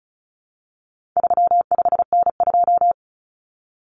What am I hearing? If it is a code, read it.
Morse code, heard as a keyed tone, about 34 wpm, 700 Hz. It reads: 35N2
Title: Common Cut Numbers - 4-Digit 35wpm